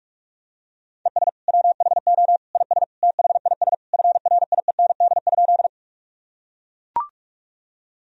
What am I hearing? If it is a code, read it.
Morse code, heard as a keyed tone, about 41 wpm, 700 Hz. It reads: ES WHO IS THIS FRIEND?
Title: Sentences from Top 300 Words and Abbreviations 45wpm